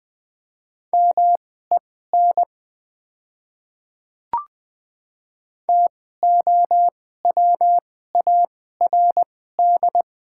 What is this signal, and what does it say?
Morse code, heard as a keyed tone, about 20 wpm, spaced as for 15 wpm, 700 Hz. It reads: MEN TOWARD
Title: 2x Character Spacing - Top 500 Words - 20wpm